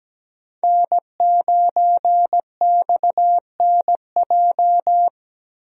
Morse code, heard as a keyed tone, about 17 wpm, 700 Hz. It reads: N9XNJ
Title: USA Call Signs - 5 Characters - Repeated 3x - 17wpm